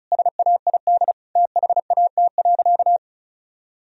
Morse code, heard as a keyed tone, about 34 wpm, 700 Hz. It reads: SAID THAT.
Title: Sentences from Top 1000 Words - Review Part 1 - 35wpm